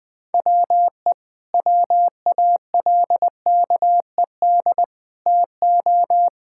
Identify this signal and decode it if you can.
Morse code, heard as a keyed tone, about 20 wpm, 700 Hz. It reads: WE WALKED TO
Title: Sentences with 5-Word Phrases - Spoken Beginning - 20wpm